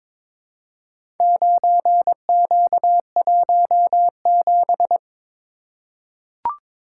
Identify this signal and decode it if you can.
Morse code, heard as a keyed tone, about 22 wpm, 700 Hz. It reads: 9Q17
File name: practice / 22wpm / Four Letter-Number Combinations - 22wpm